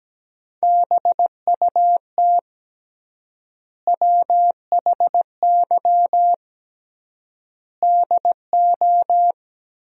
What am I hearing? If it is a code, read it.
Morse code, heard as a keyed tone, about 17 wpm, 700 Hz. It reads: BUT WHY DO
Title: 3x Word-Spacing - Sentences from Top 200 Words - 17wpm